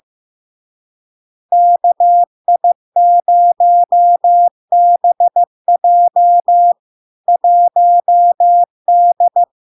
Morse code, heard as a keyed tone, about 15 wpm, 700 Hz. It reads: KI0BJ 1D
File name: LARK-FD-2022-Log-in-CW-15wpm-700-Hz